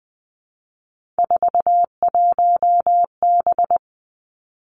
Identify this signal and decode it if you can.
Morse code, heard as a keyed tone, about 20 wpm, 700 Hz. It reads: 41B
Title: Three Challenging Letter-Number Combinations 20wpm